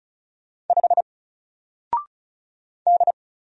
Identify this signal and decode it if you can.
Morse code, heard as a keyed tone, about 34 wpm, 700 Hz. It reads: 5 D
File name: Single-Letter-Number-35wpm